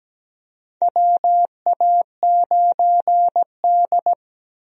Morse code, heard as a keyed tone, about 17 wpm, 700 Hz. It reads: WA9D
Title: USA Call Signs - 4 Characters - 17wpm